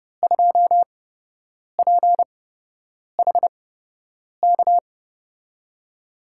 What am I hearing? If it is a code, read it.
Morse code, heard as a keyed tone, about 30 wpm, spaced as for 8 wpm, 700 Hz. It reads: 2PHK
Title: USA Call Signs - 6 Characters - 8x Character Spacing - 30wpm